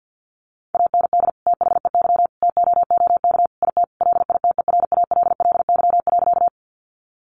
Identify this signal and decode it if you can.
Morse code, heard as a keyed tone, about 50 wpm, 700 Hz. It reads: AND THEY TOOK IT LITERALLY.
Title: Sentences from Top 1400 Words 50wpm